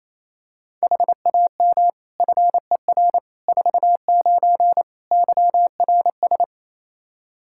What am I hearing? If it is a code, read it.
Morse code, heard as a keyed tone, about 28 wpm, 700 Hz. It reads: HAM FER 49 YRS
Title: K1USN SST QSO Practice 28wpm